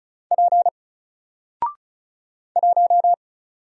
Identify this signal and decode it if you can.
Morse code, heard as a keyed tone, about 35 wpm, 700 Hz. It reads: P 1
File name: Single-Letter-Number-35wpm